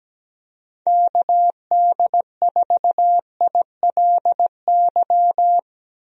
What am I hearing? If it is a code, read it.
Morse code, heard as a keyed tone, about 17 wpm, 700 Hz. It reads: KD4ILY
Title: USA Call Signs - 6 Characters - 17wpm